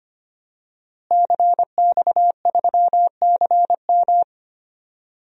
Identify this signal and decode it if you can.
Morse code, heard as a keyed tone, about 25 wpm, 700 Hz. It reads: CX3CM
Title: DXCC-100 Call Signs - 5 Characters - Repeated 3x - 25wpm